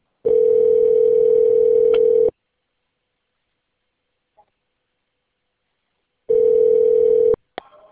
Ringback tone (North America), 440 + 480 Hz.
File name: out- - - - - .121